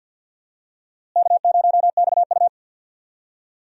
Morse code, heard as a keyed tone, about 52 wpm, 700 Hz. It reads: K0XU
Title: USA Call Signs - 4 Characters - Repeated 2x - 50wpm